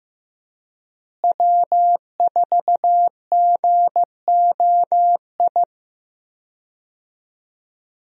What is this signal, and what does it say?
Morse code, heard as a keyed tone, about 15 wpm, 700 Hz. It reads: W4GOI
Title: USA Call Signs - 5 Characters - Repeated 3x - 15wpm